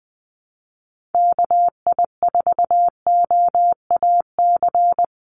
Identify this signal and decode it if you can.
Morse code, heard as a keyed tone, about 20 wpm, 700 Hz. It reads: KI4OAC
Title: USA Call Signs 20wpm